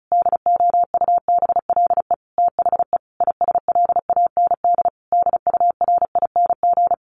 Morse code, heard as a keyed tone, about 34 wpm, 700 Hz. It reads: DOUBLE THE ISLAND DURING
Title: Sentences from Top 1400 Words 35wpm